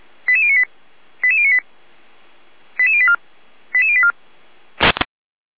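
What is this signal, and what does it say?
Five-tone selective call (ZVEI), heard as quick runs of five tones, tones of about 80 ms. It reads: 88098 88098 88084 88084
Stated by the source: Five-tone ZVEI selective call received on an SDR.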